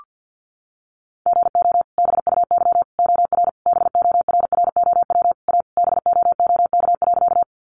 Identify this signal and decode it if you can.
Morse code, heard as a keyed tone, about 48 wpm, 700 Hz. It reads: GO BUY OR BORROW A BOOK.
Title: Sentences from Top 1500 Words 50wpm